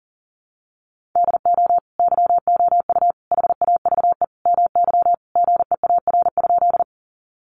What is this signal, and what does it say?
Morse code, heard as a keyed tone, about 37 wpm, 700 Hz. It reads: DO YOU HAVE MY GEAR?
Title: Sentences from Top 1400 Words 40wpm